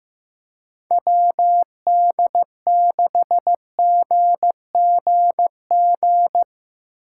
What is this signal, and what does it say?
Morse code, heard as a keyed tone, about 15 wpm, 700 Hz. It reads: WD6GGG
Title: USA Call Signs - 6 Characters - 8x Character Spacing - 15wpm